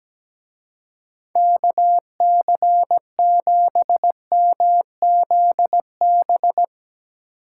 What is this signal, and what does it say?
Morse code, heard as a keyed tone, about 17 wpm, 700 Hz. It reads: KC7MZB
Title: Speed Racing - USA 2x3 Call Signs 17wpm